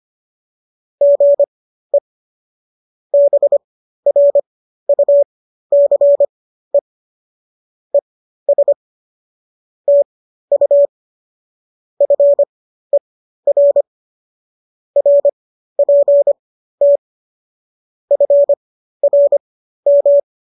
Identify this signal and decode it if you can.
Morse code, heard as a keyed tone, about 25 wpm, spaced as for 13 wpm, 570 Hz. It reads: GE BRUCE ES TU FER RPT FRM